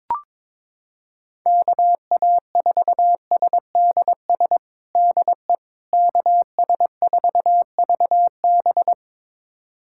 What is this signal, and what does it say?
Morse code, heard as a keyed tone, about 22 wpm, 700 Hz. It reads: KA4SDS DE KS4VB
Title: USA Callsign DE Callsign 22wpm